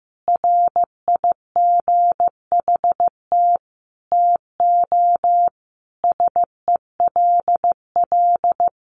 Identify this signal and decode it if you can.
Morse code, heard as a keyed tone, about 15 wpm, 700 Hz. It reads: RIGHT TO SELL
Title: Sentences from Top 1100 Words 15wpm